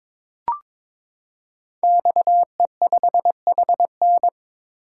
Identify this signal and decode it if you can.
Morse code, heard as a keyed tone, about 22 wpm, 700 Hz. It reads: XE5HN
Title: Code Groups - 5 Letter:Numbers - 22wpm